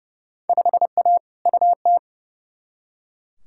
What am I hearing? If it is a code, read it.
Morse code, heard as a keyed tone, about 32 wpm, 700 Hz. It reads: 5A UT
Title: Field Day Exchanges - 30wpm